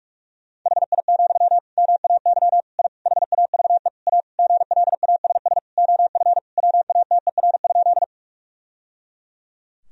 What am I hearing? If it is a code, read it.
Morse code, heard as a keyed tone, about 48 wpm, 700 Hz. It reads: HI, MAY I HAVE A GLASS OF WATER?
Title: Sentences from Top 1000 Words - Review Part 3 - 45wpm